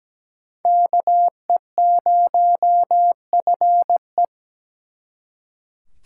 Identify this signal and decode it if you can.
Morse code, heard as a keyed tone, about 20 wpm, 700 Hz. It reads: KE0FE
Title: Speed Racing - USA 2x2 Call Signs 17wpm